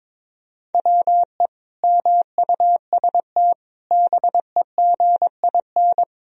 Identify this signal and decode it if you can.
Morse code, heard as a keyed tone, about 22 wpm, 700 Hz. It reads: WE MUST BEGIN
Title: Sentences from Top 300 Words and Abbreviations 22wpm